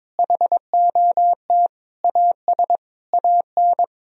Morse code, heard as a keyed tone, about 23 wpm, 700 Hz. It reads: HOT AS AN
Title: Sentences from Top 1000 Words - Review Part 4 - 22wpm